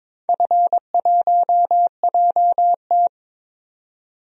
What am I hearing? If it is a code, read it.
Morse code, heard as a keyed tone, about 22 wpm, 700 Hz. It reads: F1JT
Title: DXCC-100 Call Signs - 4 Characters - 22wpm